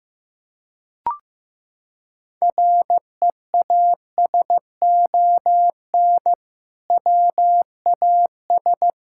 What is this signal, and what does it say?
Morse code, heard as a keyed tone, about 15 wpm, 700 Hz. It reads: REASON WAS
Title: Sentences from Top 1000 Words - Review Part 5 - 15wpm